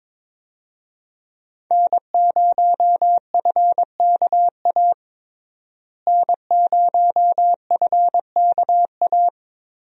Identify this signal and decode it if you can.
Morse code, heard as a keyed tone, about 22 wpm, 700 Hz. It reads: N0FKA N0FKA
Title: USA Call Signs - 5 Characters - Repeated 3x - 22wpm